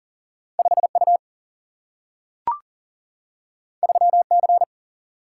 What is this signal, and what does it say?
Morse code, heard as a keyed tone, about 40 wpm, 700 Hz. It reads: 5U 3C
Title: Single-Letter Single-Number 40wpm